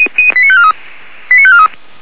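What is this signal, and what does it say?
Five-tone selective call (ZVEI), heard as quick runs of five tones, tones of about 70 ms. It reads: 89543 89543
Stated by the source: ZVEI five-tone call 89544 received by SDR, channel 464.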